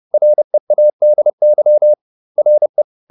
Morse code, heard as a keyed tone, about 30 wpm, 600 Hz. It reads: READY RE